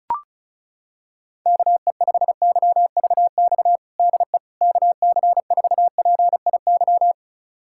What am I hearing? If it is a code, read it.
Morse code, heard as a keyed tone, about 36 wpm, 700 Hz. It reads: KE5YVX DE KC4PIY
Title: USA Callsign DE Callsign - 6 Characters - 35wpm